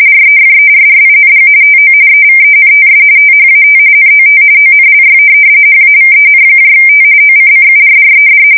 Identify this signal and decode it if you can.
RTTY, about 45.45 baud. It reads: REFERRING TO EVENTS THAT NOT ONLY HAVENT OCCURRED
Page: 170 Shift 2125/2295